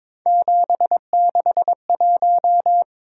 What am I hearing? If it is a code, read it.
Morse code, heard as a keyed tone, about 22 wpm, 700 Hz. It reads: 761
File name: practice / 22wpm / Three Letter-Number Combinations - 22wpm